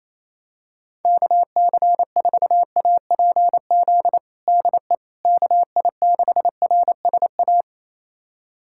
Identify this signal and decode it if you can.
Morse code, heard as a keyed tone, about 28 wpm, 700 Hz. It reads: KC4APZ DE KI6RSA
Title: USA Callsign DE Callsign - 6 Characters - 28wpm